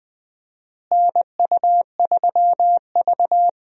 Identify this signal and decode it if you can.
Morse code, heard as a keyed tone, about 21 wpm, 700 Hz. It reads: NU3V
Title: USA Call Signs - 4 Characters - Repeated 3x - 20wpm